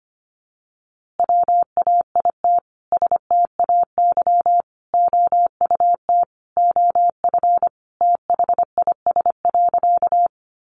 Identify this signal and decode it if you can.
Morse code, heard as a keyed tone, about 25 wpm, 700 Hz. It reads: WAIT STAY OUT OF THIS.
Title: A Sound Of Thunder 25wpm